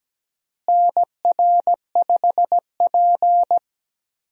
Morse code, heard as a keyed tone, about 17 wpm, 700 Hz. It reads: NR5P
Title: USA Call Signs - 4 Characters - Repeated 3x - 17wpm